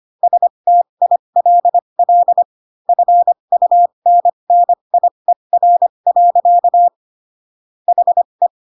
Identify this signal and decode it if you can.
Morse code, heard as a keyed tone, about 25 wpm, spaced as for 22 wpm, 700 Hz. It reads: STILL FUNNIER. HE